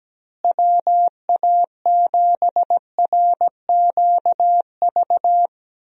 Morse code, heard as a keyed tone, about 17 wpm, 700 Hz. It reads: WA7RQV
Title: USA Call Signs - 6 Characters - Repeated 3x - 17wpm